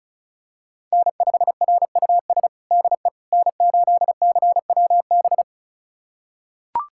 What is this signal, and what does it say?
Morse code, heard as a keyed tone, about 36 wpm, 690 Hz. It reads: N5RUS DE N8CWB
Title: USA Callsign DE Callsign - 5 Characters - 35wpm